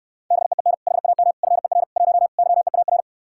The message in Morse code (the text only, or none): BEEN HAM FER 1 YEAR